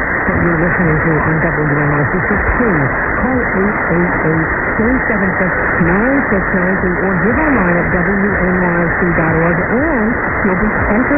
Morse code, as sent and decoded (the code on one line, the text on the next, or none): none